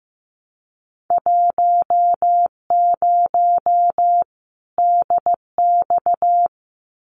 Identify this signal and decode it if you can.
Morse code, heard as a keyed tone, about 15 wpm, 700 Hz. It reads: 10 DX
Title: Winter Field Day Exchanges 15wpm